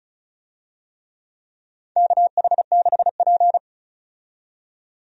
Morse code, heard as a keyed tone, about 34 wpm, 700 Hz. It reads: KH6P
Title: DXCC-100 Call Signs - 4 Characters - 35wpm